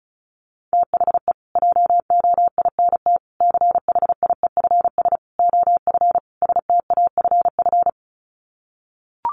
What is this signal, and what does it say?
Morse code, heard as a keyed tone, about 34 wpm, 700 Hz. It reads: THE JOINT CHIEFS OF STAFF
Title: Sets of 5 Words Encore2 35wpm